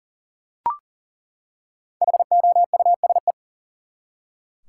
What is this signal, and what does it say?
Morse code, heard as a keyed tone, about 40 wpm, 700 Hz. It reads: HOUSE
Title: Five Letter Words - 40wpm